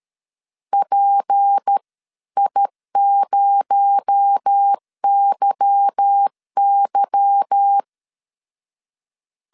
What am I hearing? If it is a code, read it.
Morse code, heard as a keyed tone, about 13 wpm, 800 Hz. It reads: PI0YY
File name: LEZIONI / 8 YJ45 / 8 Y 12wpm 2228